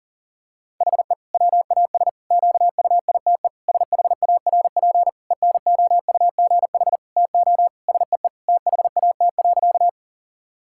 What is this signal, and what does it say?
Morse code, heard as a keyed tone, about 40 wpm, 700 Hz. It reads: HE WAS QUITE SHARP ENOUGH TO SEE THAT.
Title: Sentences from Top 1600 Words 40wpm